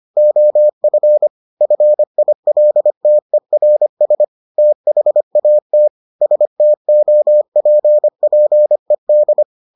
Morse code, heard as a keyed tone, about 25 wpm, 600 Hz. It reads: OF FILTERS THAT STOPPED